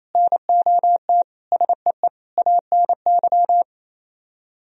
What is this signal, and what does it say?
Morse code, heard as a keyed tone, about 33 wpm, 700 Hz. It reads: NOT SEE ANY
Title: Speed-Racing - Sets of 5 Words 28wpm